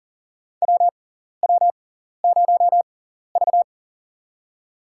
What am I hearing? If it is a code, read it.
Morse code, heard as a keyed tone, about 40 wpm, spaced as for 14 wpm, 700 Hz. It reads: WW0V
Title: USA Call Signs - 4 Characters - 6x Character Spacing - 40wpm